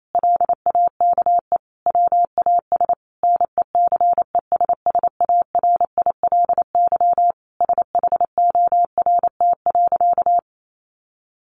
Morse code, heard as a keyed tone, about 28 wpm, 700 Hz. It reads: LAKE WAS NECESSARILY SHORT.
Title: Sentences from Top 1400 Words 28wpm